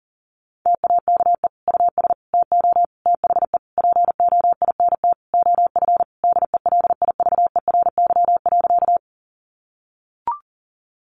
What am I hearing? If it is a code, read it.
Morse code, heard as a keyed tone, about 40 wpm, 700 Hz. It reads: TAKE US TO THE POINT OF DELIVERY.
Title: Sentences from Top 1500 Words 40wpm